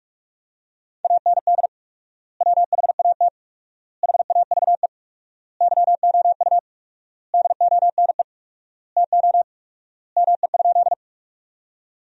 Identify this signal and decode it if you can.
Morse code, heard as a keyed tone, about 47 wpm, 700 Hz. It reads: AND WHAT HAVE YOU DONE TO ME?
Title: 4x Word-Spacing - Sentences from Top 500 Words - 45wpm